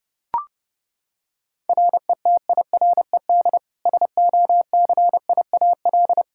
Morse code, heard as a keyed tone, about 30 wpm, 700 Hz. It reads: RETIRED SOCIAL
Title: K1USN SST QSO Practice 30wpm